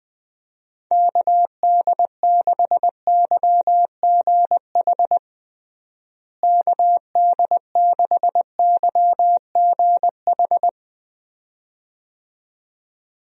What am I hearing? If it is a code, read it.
Morse code, heard as a keyed tone, about 20 wpm, 700 Hz. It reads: KD6YGH KD6YGH
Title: USA Call Signs - 6 Characters - Repeated 2x - 20wpm